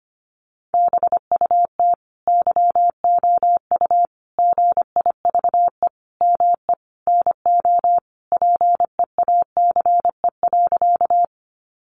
Morse code, heard as a keyed tone, about 25 wpm, 700 Hz. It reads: BUT YOU GIVE ME NO PEACE.
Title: Sentences from Top 1500 Words 25wpm